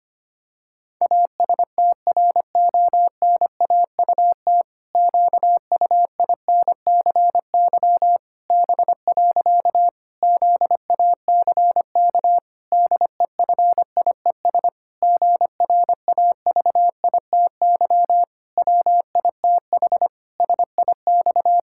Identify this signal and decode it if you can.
Morse code, heard as a keyed tone, about 25 wpm, 700 Hz. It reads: ASTRONAUT QUINCY B. ZACK DEFIES GRAVITY WITH SIX
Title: Pangram Challenge v2 - 25wpm